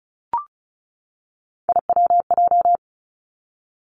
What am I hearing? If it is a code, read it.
Morse code, heard as a keyed tone, about 36 wpm, 700 Hz. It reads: IWJ
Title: Three Challenging Letter-Number Combinations 35wpm